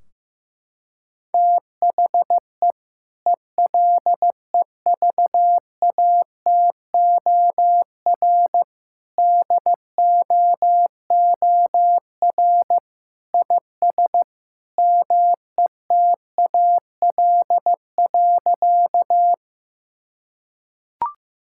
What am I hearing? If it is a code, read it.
Morse code, heard as a keyed tone, about 16 wpm, 700 Hz. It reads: THE ELEVATOR DOOR IS METAL.
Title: Sentences from Top 1600 Words 15wpm